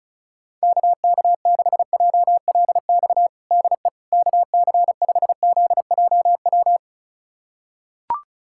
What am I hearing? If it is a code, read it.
Morse code, heard as a keyed tone, about 36 wpm, 700 Hz. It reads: KK6JLX DE KC5ZJW
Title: USA Callsign DE Callsign 35wpm